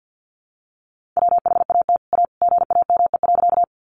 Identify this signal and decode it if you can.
Morse code, heard as a keyed tone, about 50 wpm, 700 Hz. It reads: WHAT A GAME.
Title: Sentences from Top 500 Words 50wpm